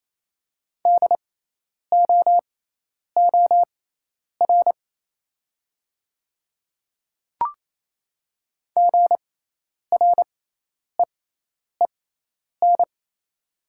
Morse code, heard as a keyed tone, about 28 wpm, spaced as for 10 wpm, 700 Hz. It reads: DOOR GREEN
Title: 6x Character Spacing - Top 500 Words - 28wpm